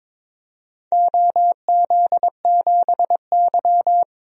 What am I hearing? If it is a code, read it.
Morse code, heard as a keyed tone, about 22 wpm, 700 Hz. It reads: OZ7Y
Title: DXCC-100 Call Signs - 4 Characters - 22wpm